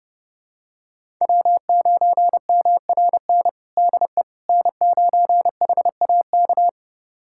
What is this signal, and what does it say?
Morse code, heard as a keyed tone, about 29 wpm, 700 Hz. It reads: W9MRN DE N9HAK
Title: USA Callsign DE Callsign - 5 Characters - 30wpm